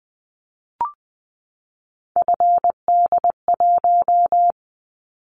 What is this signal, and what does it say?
Morse code, heard as a keyed tone, about 20 wpm, 700 Hz. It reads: FD1
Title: Callsign Prefixes for Top 50 DXCC - 20wpm